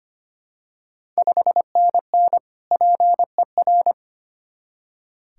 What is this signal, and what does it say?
Morse code, heard as a keyed tone, about 25 wpm, 700 Hz. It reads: 5NN PER
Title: Ontario QSO Party Exchange 25wpm